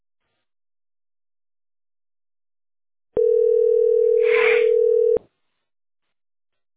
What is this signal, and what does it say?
Ringback tone (North America), 440 + 480 Hz.